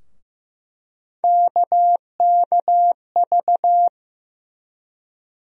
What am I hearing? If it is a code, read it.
Morse code, heard as a keyed tone, about 15 wpm, 700 Hz. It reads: KKV
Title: USA Call Sign Postfixes - 3 Letters 15wpm